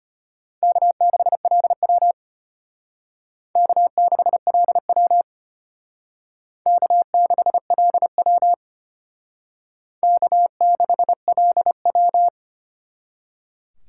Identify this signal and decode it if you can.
Morse code, heard as a keyed tone, about 32 wpm, 700 Hz. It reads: K6LW K6LW K6LW K6LW